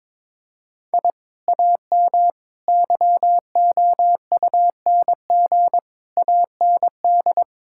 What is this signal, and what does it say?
Morse code, heard as a keyed tone, about 22 wpm, 700 Hz. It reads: I AM YOUNG AND
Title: Sentences from Top 1000 Words - Review Part 5 - 22wpm